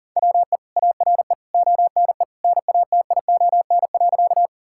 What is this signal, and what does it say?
Morse code, heard as a keyed tone, about 40 wpm, 700 Hz. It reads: WE ARE ONE NATION.
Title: Sentences from Top 1000 Words - Review Part 6 - 40wpm